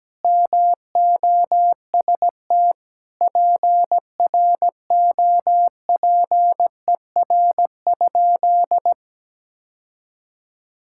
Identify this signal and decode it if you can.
Morse code, heard as a keyed tone, about 17 wpm, 700 Hz. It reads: MOST PROPER?
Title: Sentences from Top 1000 Words - Review Part 5 - 17wpm